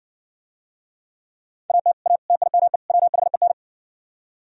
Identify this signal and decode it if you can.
Morse code, heard as a keyed tone, about 65 wpm, 700 Hz. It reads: AT A TIME WHEN